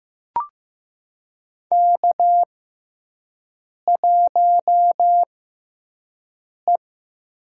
Morse code, heard as a keyed tone, about 15 wpm, spaced as for 5 wpm, 700 Hz. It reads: K1E
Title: USA Call Signs - 6x Character Spacing - 15wpm